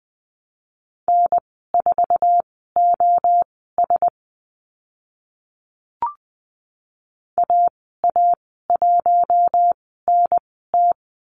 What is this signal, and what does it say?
Morse code, heard as a keyed tone, about 20 wpm, spaced as for 14 wpm, 700 Hz. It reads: N4OS AA1NT
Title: 055 Lesson 10g - Letter r - Call Signs 20wpm